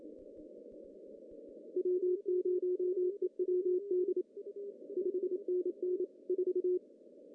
Morse code, heard as a keyed tone, about 27 wpm, 370 Hz, with radio noise. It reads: W0EWD 5NN 4
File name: CW / cqww-cw-k5tr